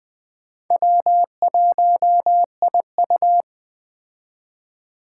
Morse code, heard as a keyed tone, about 22 wpm, 700 Hz. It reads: W1IU